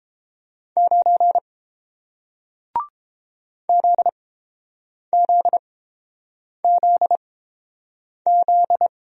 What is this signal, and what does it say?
Morse code, heard as a keyed tone, about 30 wpm, spaced as for 7 wpm, 700 Hz. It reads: 9 ZZZZ